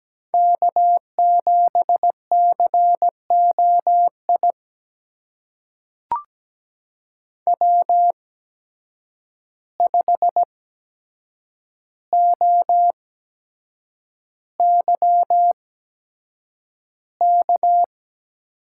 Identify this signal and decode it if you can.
Morse code, heard as a keyed tone, about 17 wpm, spaced as for 7 wpm, 700 Hz. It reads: K7COI W5OYK
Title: USA Call Signs - 5 Characters - 8x Character Spacing - 17wpm